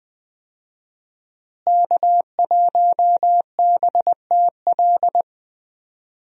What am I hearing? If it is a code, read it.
Morse code, heard as a keyed tone, about 21 wpm, 700 Hz. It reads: K1BTL
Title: USA Call Signs - 5 Characters - Repeated 2x - 20wpm